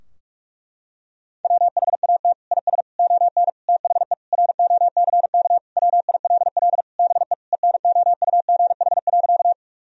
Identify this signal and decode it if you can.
Morse code, heard as a keyed tone, about 41 wpm, 700 Hz. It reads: WHAT IS ON THE ROCK WILL BE ENOUGH.
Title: Sentences from Top 1000 Words - Review Part 2 - 45wpm